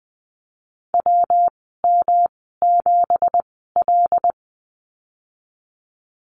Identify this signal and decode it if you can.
Morse code, heard as a keyed tone, about 20 wpm, spaced as for 15 wpm, 700 Hz. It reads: WM7L
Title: 219 Lesson 35g - Number 0 - Call Signs 20wpm